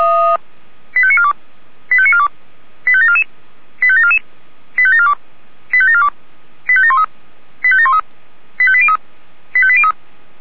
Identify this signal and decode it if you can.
Five-tone selective call (ZVEI), heard as quick runs of five tones, tones of about 70 ms. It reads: 85942 85942 85744 85744 85742 85742 85713 85713 85803 85803